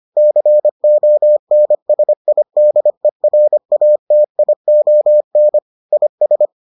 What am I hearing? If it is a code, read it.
Morse code, heard as a keyed tone, about 25 wpm, 600 Hz. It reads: CONSIDERATION IS